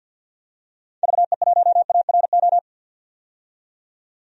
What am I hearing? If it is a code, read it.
Morse code, heard as a keyed tone, about 50 wpm, 700 Hz. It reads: VE1ARO
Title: DXCC-100 Call Signs - 6 Characters - 50wpm